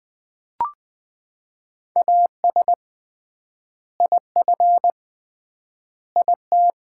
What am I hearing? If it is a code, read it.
Morse code, heard as a keyed tone, about 21 wpm, 700 Hz. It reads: AS IF IT
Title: 3x Word-Spacing - Sets of 5 Words - 20wpm